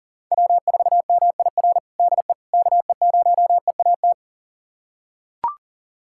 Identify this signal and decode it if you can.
Morse code, heard as a keyed tone, about 37 wpm, 700 Hz. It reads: W4MIR DE KE0EAT
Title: USA Callsign DE Callsign 40wpm